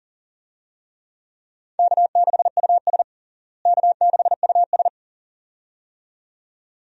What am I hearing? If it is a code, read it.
Morse code, heard as a keyed tone, about 37 wpm, 700 Hz. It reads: K6US K6US
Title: USA Call Signs - 4 Characters - Repeated 2x - 40wpm